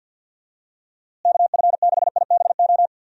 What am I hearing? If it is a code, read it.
Morse code, heard as a keyed tone, about 48 wpm, 700 Hz. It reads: KF6IBO
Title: USA Call Signs - 6 Characters - Repeated 2x - 50wpm